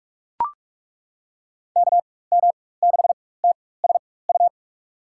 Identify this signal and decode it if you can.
Morse code, heard as a keyed tone, about 43 wpm, spaced as for 21 wpm, 700 Hz. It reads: KM6TSU
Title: USA Call Signs - 4x Character Spacing - 45wpm